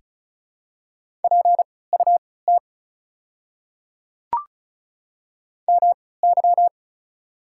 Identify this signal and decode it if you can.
Morse code, heard as a keyed tone, about 34 wpm, spaced as for 20 wpm, 700 Hz. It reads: PUT MY
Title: 3x Character Spacing - Top 200 Words - 35wpm